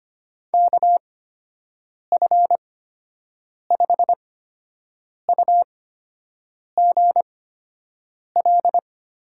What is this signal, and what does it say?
Morse code, heard as a keyed tone, about 25 wpm, spaced as for 7 wpm, 700 Hz. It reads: KF5UGL
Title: USA Call Signs - 6 Characters - 8x Character Spacing - 25wpm